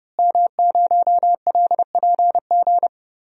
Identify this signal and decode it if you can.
Morse code, heard as a keyed tone, about 30 wpm, 700 Hz. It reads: M0LPG
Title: DXCC-100 Call Signs - 5 Characters - Repeated 2x - 30wpm